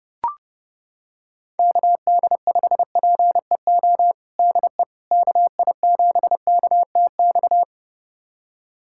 Morse code, heard as a keyed tone, about 30 wpm, 700 Hz. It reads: KD5PEO DE KI7KTX
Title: USA Callsign DE Callsign - 6 Characters - 30wpm